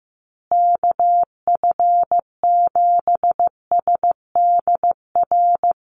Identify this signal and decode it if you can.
Morse code, heard as a keyed tone, about 15 wpm, 700 Hz. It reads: KF7SDR
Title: USA Call Signs Repeated 2x 15wpm